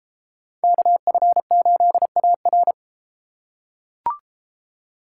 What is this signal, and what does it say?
Morse code, heard as a keyed tone, about 32 wpm, 700 Hz. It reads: KF8AR